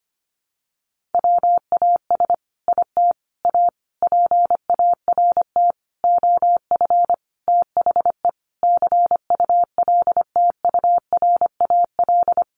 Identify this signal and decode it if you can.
Morse code, heard as a keyed tone, about 25 wpm, 700 Hz. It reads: WAS IT A PART OF THE CULTURAL
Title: Sentences from Top 1500 Words 25wpm